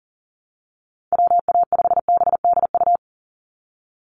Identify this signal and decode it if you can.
Morse code, heard as a keyed tone, about 37 wpm, 700 Hz. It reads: WA5BDU
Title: Call Signs - 40wpm